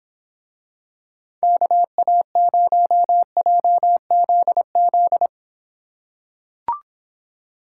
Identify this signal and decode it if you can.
Morse code, heard as a keyed tone, about 25 wpm, 700 Hz. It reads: KA0JZZ